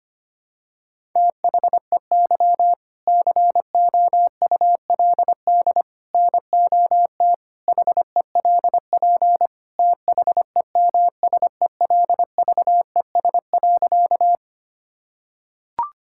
Morse code, heard as a keyed tone, about 25 wpm, 700 Hz. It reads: THEY COULD NOT HELP THEMSELVES.
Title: Sentences from Top 1000 Words - Review Part 5 - 25wpm